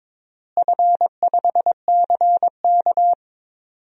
Morse code, heard as a keyed tone, about 22 wpm, 700 Hz. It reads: F5CK
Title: DXCC-100 Call Signs - 4 Characters - Repeated 3x - 22wpm